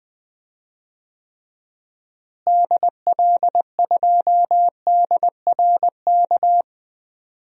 Morse code, heard as a keyed tone, about 20 wpm, 700 Hz. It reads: DL2DRK
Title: DXCC-100 Call Signs - 6 Characters - Repeated 2x - 20wpm